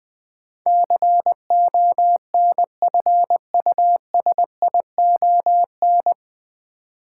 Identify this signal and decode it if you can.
Morse code, heard as a keyed tone, about 21 wpm, 700 Hz. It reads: CONFUSION
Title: Sentences from Top 1600 Words - New Word Drill - 20wpm